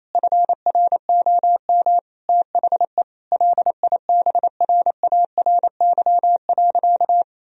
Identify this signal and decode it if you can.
Morse code, heard as a keyed tone, about 28 wpm, 700 Hz. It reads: FROM THE LIBRARY.
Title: Sentences from Top 1000 Words - Review Part 4 - 28wpm